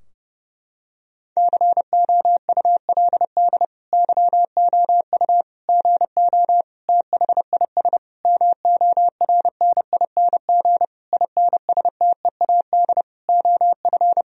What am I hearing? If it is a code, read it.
Morse code, heard as a keyed tone, about 29 wpm, 700 Hz. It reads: COULD YOU GO THIS MORNING INSTEAD OF
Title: Sentences from Top 1000 Words - Review Part 4 - 30wpm